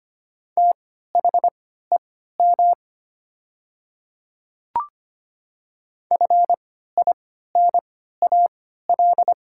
Morse code, heard as a keyed tone, about 25 wpm, spaced as for 14 wpm, 700 Hz. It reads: THEM FINAL
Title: 3x Character Spacing - Top 500 Words - 25wpm